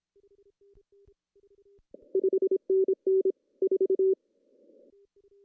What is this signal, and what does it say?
Morse code, heard as a keyed tone, about 26 wpm, 380 Hz. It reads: 5NN 4
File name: CW / cqww-cw-k5tr